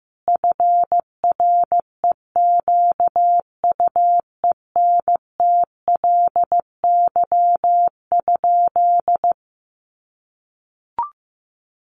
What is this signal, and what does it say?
Morse code, heard as a keyed tone, about 15 wpm, 700 Hz. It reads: FREQUENTLY?
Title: Sentences from Top 1300 Words 15wpm